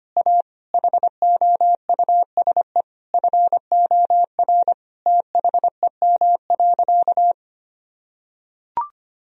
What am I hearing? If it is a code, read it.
Morse code, heard as a keyed tone, about 25 wpm, 700 Hz. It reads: A HOUSE FOR THEM.
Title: Sentences from Top 1000 Words - Review Part 1 - 25wpm